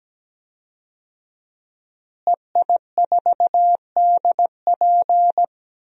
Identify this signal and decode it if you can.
Morse code, heard as a keyed tone, about 17 wpm, 700 Hz. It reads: EI4DP
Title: DXCC-100 Call Signs - 5 Characters - Repeated 2x - 17wpm